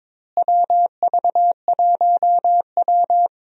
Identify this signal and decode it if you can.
Morse code, heard as a keyed tone, about 22 wpm, 700 Hz. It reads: WV1W
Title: USA Call Signs - 4 Characters - Repeated 3x - 22wpm